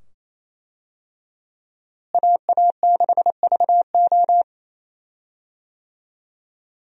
Morse code, heard as a keyed tone, about 28 wpm, 700 Hz. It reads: AA6VO
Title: USA Call Signs - 5 Characters - Repeated 2x - 28wpm